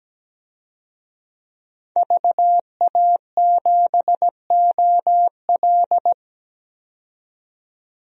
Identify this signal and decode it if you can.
Morse code, heard as a keyed tone, about 17 wpm, 700 Hz. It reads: VA7OL
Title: DXCC-100 Call Signs - 5 Characters - 17wpm